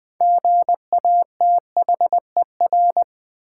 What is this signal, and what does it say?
Morse code, heard as a keyed tone, about 20 wpm, 700 Hz. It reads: GATHER
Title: Six Letter Words - 20wpm